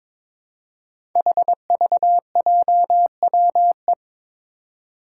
Morse code, heard as a keyed tone, about 21 wpm, 700 Hz. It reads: HVJWE
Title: Five Challenging Letter-Number Combinations 22wpm